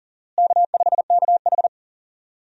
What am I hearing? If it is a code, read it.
Morse code, heard as a keyed tone, about 40 wpm, 700 Hz. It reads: K5KH
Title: USA Call Signs - 4 Characters - Repeated 2x - 40wpm